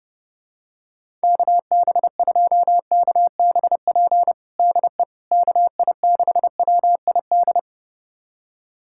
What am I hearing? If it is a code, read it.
Morse code, heard as a keyed tone, about 32 wpm, 700 Hz. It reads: KB2KBP DE KI6WID
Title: USA Callsign DE Callsign - 6 Characters - 30wpm